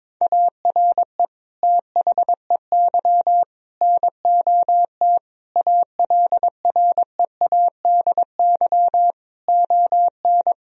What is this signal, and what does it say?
Morse code, heard as a keyed tone, about 22 wpm, 700 Hz. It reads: ARE THEY NOT ALREADY ON
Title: Sentences from Top 1000 Words - Review Part 1 - 22wpm